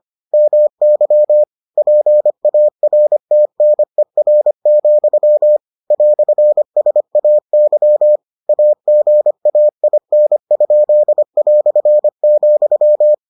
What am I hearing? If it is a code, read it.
Morse code, heard as a keyed tone, about 25 wpm, 600 Hz. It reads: MY PARTNER, "SAY AGAIN?",